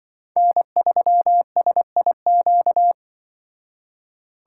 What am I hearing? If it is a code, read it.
Morse code, heard as a keyed tone, about 25 wpm, 700 Hz. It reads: N3SIQ
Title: Speed Racing - USA 1x3 Call Signs 20wpm